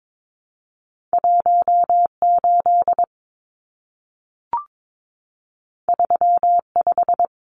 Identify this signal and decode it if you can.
Morse code, heard as a keyed tone, about 22 wpm, 700 Hz. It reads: 18 35
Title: Numbers - 2-Digit 22wpm